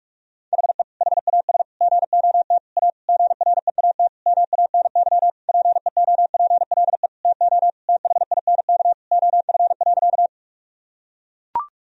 Morse code, heard as a keyed tone, about 41 wpm, 700 Hz. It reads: HE HAS GOT A GREAT MANY PEOPLE TO THINK OF.
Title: Sentences from Top 1000 Words - Review Part 1 - 45wpm